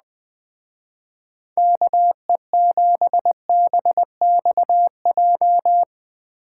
Morse code, heard as a keyed tone, about 19 wpm, 700 Hz. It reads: KE7BXJ
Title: Speed Racing - USA Call Signs 17wpm